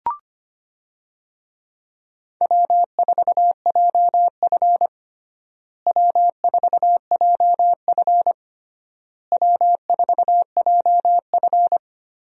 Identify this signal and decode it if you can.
Morse code, heard as a keyed tone, about 25 wpm, 700 Hz. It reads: W4JF W4JF W4JF
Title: USA Call Signs - 4 Characters - Repeated 3x - 25wpm